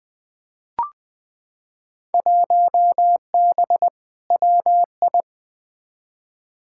Morse code, heard as a keyed tone, about 20 wpm, 700 Hz. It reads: 1B WI
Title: Field Day Exchanges - 20wpm